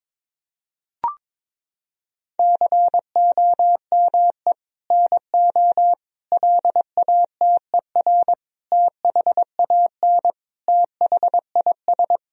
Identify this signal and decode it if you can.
Morse code, heard as a keyed tone, about 22 wpm, 700 Hz. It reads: COME NO LATER THAN THIS
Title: Sentences from Top 1000 Words - Review Part 4 - 22wpm